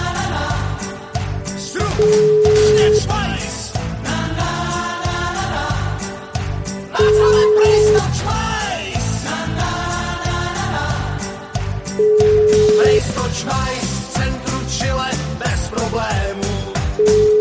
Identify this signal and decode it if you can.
Ringback tone (Europe, 425 Hz), with noise.